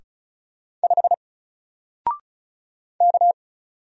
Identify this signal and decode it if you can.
Morse code, heard as a keyed tone, about 36 wpm, 700 Hz. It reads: 5 K
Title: Single Letter-Number Alt 35wpm